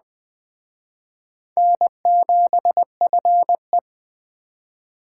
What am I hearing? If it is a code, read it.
Morse code, heard as a keyed tone, about 23 wpm, 700 Hz. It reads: N7FE